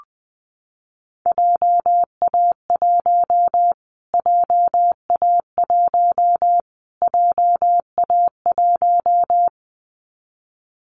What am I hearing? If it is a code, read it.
Morse code, heard as a keyed tone, about 20 wpm, 700 Hz. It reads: JA1 JA1 JA1
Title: Callsign Prefixes for Top 25 DXCC - Repeated 3x - 20wpm